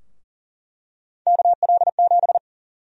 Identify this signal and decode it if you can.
Morse code, heard as a keyed tone, about 37 wpm, 700 Hz. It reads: KL7
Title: USA Call Sign Prefixes - 40wpm